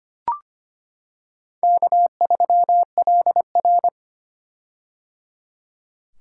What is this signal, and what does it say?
Morse code, heard as a keyed tone, about 25 wpm, 700 Hz. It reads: K3LR
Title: K1USN SST QSO Practice - 25wpm